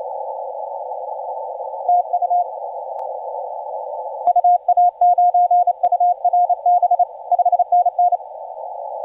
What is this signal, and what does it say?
Morse code, heard as a keyed tone, about 30 wpm, 710 Hz. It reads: TU UA9URB 5NN
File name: 9N7EI 17CW